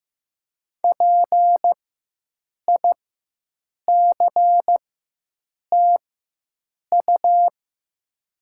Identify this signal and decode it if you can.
Morse code, heard as a keyed tone, about 15 wpm, spaced as for 7 wpm, 700 Hz. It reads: PICTU
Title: 4x Character Spacing - Top 200 Words - 15wpm